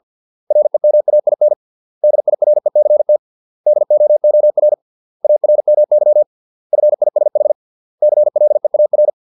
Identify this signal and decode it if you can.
Morse code, heard as a keyed tone, about 50 wpm, 600 Hz. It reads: REMAIN DIRECT DOOR ARMY FISH CLEAR